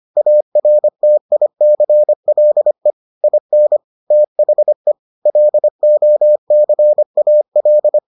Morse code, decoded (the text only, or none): ARTICLE IN THE LOCAL